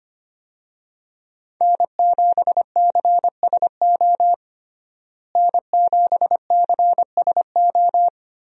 Morse code, heard as a keyed tone, about 25 wpm, 700 Hz. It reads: N7CSO N7CSO
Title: USA Call Signs - 5 Characters - Repeated 3x - 25wpm